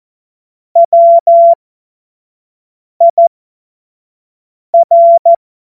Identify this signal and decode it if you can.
Morse code, heard as a keyed tone, about 14 wpm, spaced as for 5 wpm, 680 Hz. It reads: WIR
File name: Morsekurs / Witze / Witz 22. November 2022 25 F70